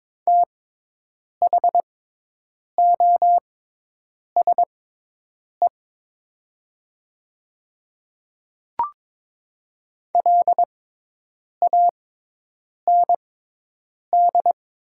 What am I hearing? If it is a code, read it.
Morse code, heard as a keyed tone, about 22 wpm, spaced as for 8 wpm, 700 Hz. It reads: THOSE LAND
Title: 6x Character Spacing - Top 300 Words - 22wpm